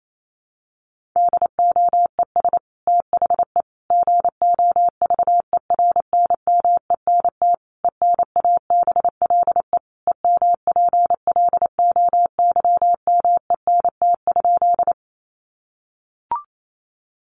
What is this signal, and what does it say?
Morse code, heard as a keyed tone, about 28 wpm, 700 Hz. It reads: DOES THE GOVERNMENT ENABLE EMPLOYMENT?
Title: Sentences from Top 1500 Words 28wpm